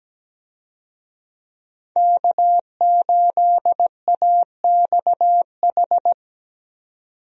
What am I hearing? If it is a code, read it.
Morse code, heard as a keyed tone, about 17 wpm, 700 Hz. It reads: K8AXH
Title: USA Call Signs - 5 Characters - 17wpm